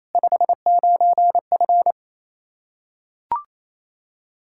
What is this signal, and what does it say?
Morse code, heard as a keyed tone, about 28 wpm, 700 Hz. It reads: 59F
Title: Practice QSO - Exchange 2 - Info - 28wpm